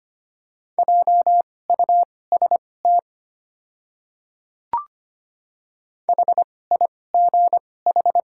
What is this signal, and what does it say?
Morse code, heard as a keyed tone, about 25 wpm, spaced as for 18 wpm, 700 Hz. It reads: JUST HIGH
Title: 2x Character Spacing - Top 300 Words - 25wpm